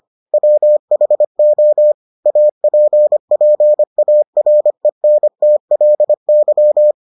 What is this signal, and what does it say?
Morse code, heard as a keyed tone, about 25 wpm, 600 Hz. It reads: WHO APPARENTLY